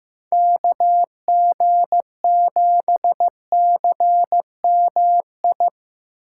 Morse code, heard as a keyed tone, about 15 wpm, 700 Hz. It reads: KG7CMI
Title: USA Call Signs - 6 Characters - 8x Character Spacing - 15wpm